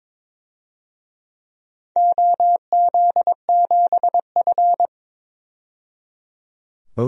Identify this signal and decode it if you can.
Morse code, heard as a keyed tone, about 22 wpm, 700 Hz. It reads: OZ7F
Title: DXCC-100 Call Signs - 4 Characters - 22wpm